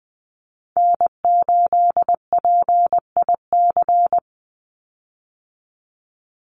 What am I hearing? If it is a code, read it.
Morse code, heard as a keyed tone, about 20 wpm, 700 Hz. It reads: N8PIC
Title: USA Call Signs 20wpm